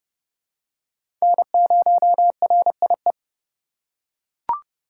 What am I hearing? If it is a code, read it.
Morse code, heard as a keyed tone, about 30 wpm, 700 Hz. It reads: N0RIE
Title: USA Call Signs - 5 Characters - 2x Character Spacing - 30wpm